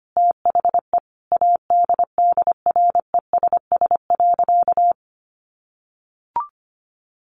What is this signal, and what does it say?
Morse code, heard as a keyed tone, about 25 wpm, 700 Hz. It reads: THE ADDRESS.
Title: Sentences from Top 1000 Words 25wpm